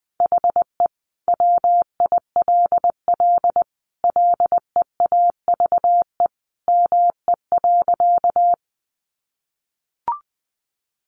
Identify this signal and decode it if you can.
Morse code, heard as a keyed tone, about 20 wpm, 700 Hz. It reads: HE WILL LEAVE ME.
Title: Sentences from Top 400 Words 20wpm